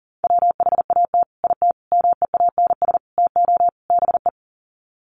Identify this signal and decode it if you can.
Morse code, heard as a keyed tone, about 37 wpm, 700 Hz. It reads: WHAT IT MEANS TO BE
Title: Sets of 5 Words 40wpm